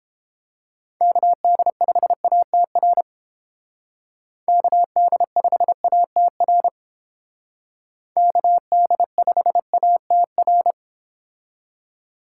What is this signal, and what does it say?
Morse code, heard as a keyed tone, about 29 wpm, 700 Hz. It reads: KD5ATR KD5ATR KD5ATR